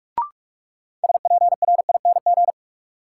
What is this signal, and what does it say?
Morse code, heard as a keyed tone, about 48 wpm, 700 Hz. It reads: SPRING
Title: Sentences from Top 1200 Words - New Word Drill Alt - 45wpm